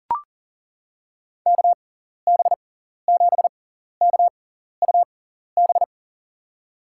Morse code, heard as a keyed tone, about 40 wpm, spaced as for 14 wpm, 700 Hz. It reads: KB7KUB
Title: USA Call Signs - 6 Characters - 6x Character Spacing - 40wpm